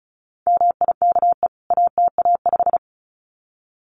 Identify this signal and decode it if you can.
Morse code, heard as a keyed tone, about 34 wpm, 700 Hz. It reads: MIKE ATA5
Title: CWT Exchange 35wpm